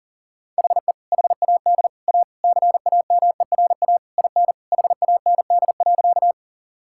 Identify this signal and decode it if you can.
Morse code, heard as a keyed tone, about 40 wpm, 700 Hz. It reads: HE HAD A CAMERA IN HAND.
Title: Sentences from Top 1000 Words - Review Part 5 - 40wpm